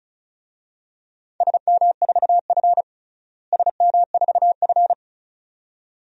Morse code, heard as a keyed tone, about 36 wpm, 700 Hz. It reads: SM4F SM4F
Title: DXCC-100 Call Signs - 4 Characters - Repeated 2x - 35wpm